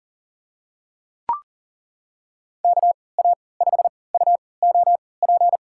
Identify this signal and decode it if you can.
Morse code, heard as a keyed tone, about 40 wpm, spaced as for 23 wpm, 700 Hz. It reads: KA5UOP
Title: USA Call Signs - 6 Characters - 3x Character Spacing - 40wpm